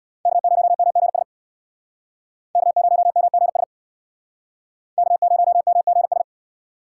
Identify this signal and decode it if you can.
Morse code, heard as a keyed tone, about 63 wpm, 700 Hz. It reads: K0MGS K0MGS K0MGS